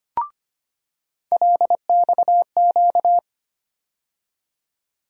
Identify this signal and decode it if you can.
Morse code, heard as a keyed tone, about 25 wpm, 700 Hz. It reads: LXQ
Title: USA Call Sign Postfixes - 3 Letters 25wpm